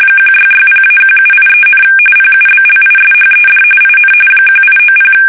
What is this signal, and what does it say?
RTTY, about 45.45 baud. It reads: 1660-1783, PUBLISHED IN 1890
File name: fsk850 / usnavy3